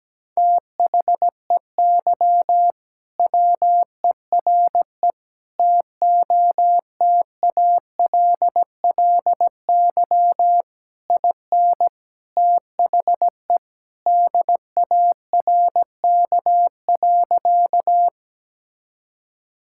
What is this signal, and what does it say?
Morse code, heard as a keyed tone, about 17 wpm, 700 Hz. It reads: THEY WERE TOTALLY IN THE DARK.
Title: Sentences from Top 1600 Words 17wpm